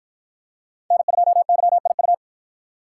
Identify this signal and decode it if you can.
Morse code, heard as a keyed tone, about 53 wpm, 700 Hz. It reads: N2YIU